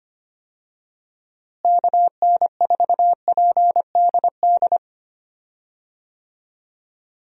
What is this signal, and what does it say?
Morse code, heard as a keyed tone, about 25 wpm, 700 Hz. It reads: KN4PDD
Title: USA Call Signs - 6 Characters - Repeated 2x - 25wpm